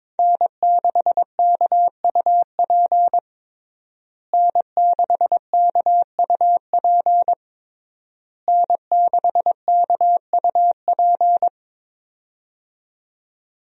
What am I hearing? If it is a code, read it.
Morse code, heard as a keyed tone, about 22 wpm, 700 Hz. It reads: N6KUP N6KUP N6KUP
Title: USA Call Signs - 5 Characters - Repeated 3x - 22wpm